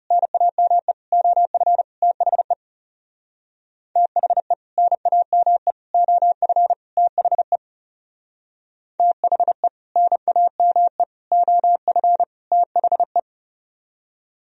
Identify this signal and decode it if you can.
Morse code, heard as a keyed tone, about 36 wpm, 700 Hz. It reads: NAME OF THE THE NAME OF THE THE NAME OF THE
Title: Speed-Racing - Sets of 4 Words 30wpm